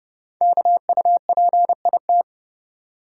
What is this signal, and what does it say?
Morse code, heard as a keyed tone, about 29 wpm, 700 Hz. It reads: KUPIT
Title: Code Groups - 5 Letter:Numbers - 30wpm